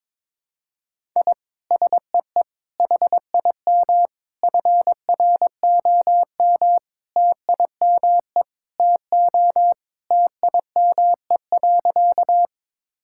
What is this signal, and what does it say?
Morse code, heard as a keyed tone, about 21 wpm, 700 Hz. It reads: I SEE HIM FROM TIME TO TIME.
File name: practice / 22wpm / Sentences with 4-Word Phrases - 22wpm